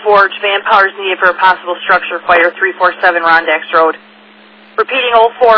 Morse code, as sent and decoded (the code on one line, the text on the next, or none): none